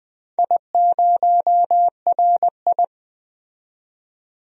Morse code, heard as a keyed tone, about 20 wpm, 700 Hz. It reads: I0RI
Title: DXCC-100 Call Signs - 4 Characters - 20wpm